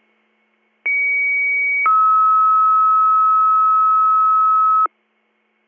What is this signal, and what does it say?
Two-tone page, heard as a one-second tone, then a three-second tone, about 2250 then 1290 Hz. It is heard from 0.9 to 4.9 s.